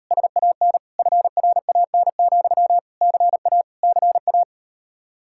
Morse code, heard as a keyed tone, about 39 wpm, 700 Hz. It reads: SAN FRAN, CA CA